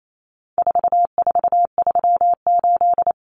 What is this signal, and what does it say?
Morse code, heard as a keyed tone, about 28 wpm, 700 Hz. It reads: 4438
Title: Numbers - 4-Digit 28wpm